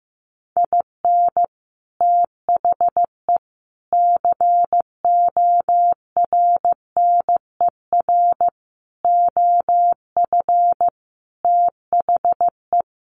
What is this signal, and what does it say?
Morse code, heard as a keyed tone, about 15 wpm, 700 Hz. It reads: IN THE CORNER OF THE
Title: Sets of 5 Words Encore 15wpm